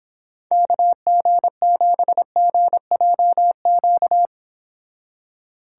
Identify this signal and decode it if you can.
Morse code, heard as a keyed tone, about 26 wpm, 700 Hz. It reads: KG7GJQ